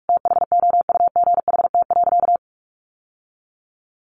Morse code, heard as a keyed tone, about 44 wpm, 700 Hz. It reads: THOUGHT.
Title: Sentences from Top 800 Words 45wpm